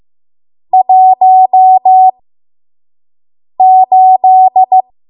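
Morse code, heard as a keyed tone, about 15 wpm, spaced as for 5 wpm, 750 Hz. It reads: 18
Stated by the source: 5 WPM Morse code practice.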